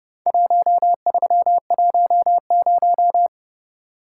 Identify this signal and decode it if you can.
Morse code, heard as a keyed tone, about 29 wpm, 700 Hz. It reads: 1310